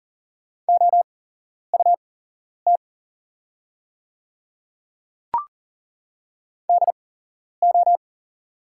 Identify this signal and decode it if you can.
Morse code, heard as a keyed tone, about 41 wpm, spaced as for 11 wpm, 700 Hz. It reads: OUT DO
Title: 8x Character Spacing - Top 100 Words - 40wpm